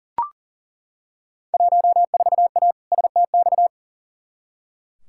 Morse code, heard as a keyed tone, about 40 wpm, 700 Hz. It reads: 14A STX
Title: Field Day Exchanges Encore - 40wpm